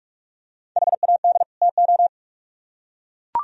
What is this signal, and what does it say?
Morse code, heard as a keyed tone, about 42 wpm, 700 Hz. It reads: HAD TO